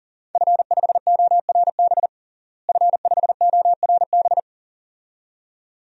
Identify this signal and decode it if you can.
Morse code, heard as a keyed tone, about 40 wpm, 700 Hz. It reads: F5ORB F5ORB
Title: DXCC-100 Call Signs - 5 Characters - Repeated 2x - 40wpm